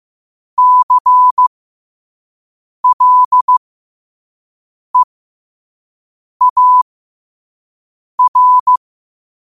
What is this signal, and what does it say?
Morse code, heard as a keyed tone, about 15 wpm, spaced as for 5 wpm, 1000 Hz. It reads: CLEAR
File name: qotdfiles / 5wpm